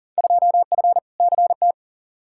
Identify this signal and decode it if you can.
Morse code, heard as a keyed tone, about 39 wpm, 700 Hz. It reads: 2F CT
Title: Field Day Exchanges Encore - 40wpm